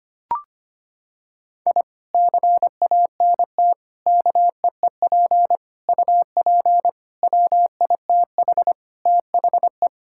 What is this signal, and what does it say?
Morse code, heard as a keyed tone, about 25 wpm, 700 Hz. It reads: I CANT KEEP UP WITH THE
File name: practice / 25wpm / Sentences with 2-Word Phrases - 25wpm